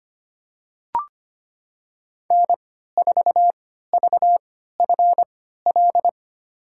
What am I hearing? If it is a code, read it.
Morse code, heard as a keyed tone, about 25 wpm, spaced as for 14 wpm, 700 Hz. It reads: N4VFL
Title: USA Call Signs - 3x Character Spacing - 25wpm